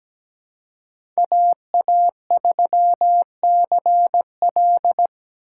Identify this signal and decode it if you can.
Morse code, heard as a keyed tone, about 17 wpm, 700 Hz. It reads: AA3CL
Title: USA Call Signs - 5 Characters - 3x Character Spacing - 17wpm